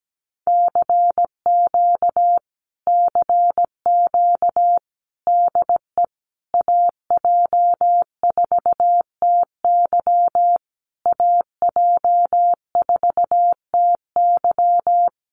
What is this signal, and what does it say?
Morse code, heard as a keyed tone, about 17 wpm, 700 Hz. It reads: CQ CQ DE AJ4TY AJ4TY
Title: Common QSO Phrases Encore 17wpm